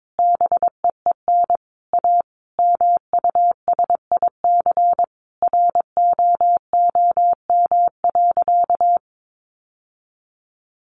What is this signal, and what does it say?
Morse code, heard as a keyed tone, about 22 wpm, 700 Hz. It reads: BEEN A MUSIC ROOM.
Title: Sentences from Top 700 Words 22wpm